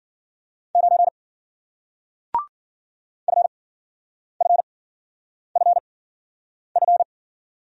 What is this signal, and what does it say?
Morse code, heard as a keyed tone, about 52 wpm, 700 Hz. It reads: 9 F F F F